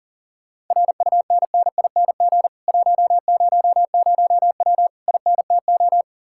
Morse code, heard as a keyed tone, about 50 wpm, 700 Hz. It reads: RUNNING 100W INTO
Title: Speed-Racing Common QSO Phrases - 40wpm